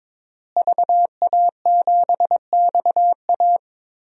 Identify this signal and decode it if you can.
Morse code, heard as a keyed tone, about 21 wpm, 700 Hz. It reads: VA7XA
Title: DXCC-100 Call Signs - 5 Characters - Repeated 3x - 22wpm